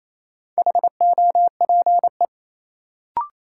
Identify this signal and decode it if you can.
Morse code, heard as a keyed tone, about 28 wpm, 700 Hz. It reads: HOPE
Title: Sentences from Top 1000 Words - New Word Drill - 28wpm